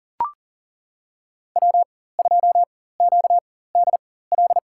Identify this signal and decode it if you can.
Morse code, heard as a keyed tone, about 40 wpm, spaced as for 19 wpm, 700 Hz. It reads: W2QDL
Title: USA Call Signs - 5 Characters - 4x Character Spacing - 40wpm